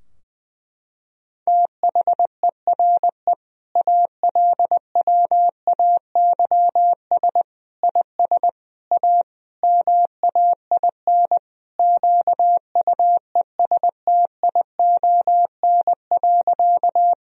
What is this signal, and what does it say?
Morse code, heard as a keyed tone, about 19 wpm, 700 Hz. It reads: THERE ALWAYS IS A MAIN QUESTION.
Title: Sentences from Top 1000 Words - Review Part 2 - 20wpm